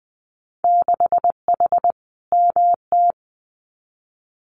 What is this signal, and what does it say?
Morse code, heard as a keyed tone, about 20 wpm, 700 Hz. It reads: 6H MT
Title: Winter Field Day Exchanges Encore 20wpm